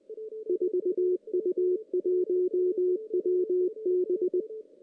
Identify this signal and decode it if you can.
Morse code, heard as a keyed tone, about 22 wpm, 370 Hz, with radio noise. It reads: 4U1WB
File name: 2016 JAN CW NAQP / 2016 Jan naqp k5ot - 0-5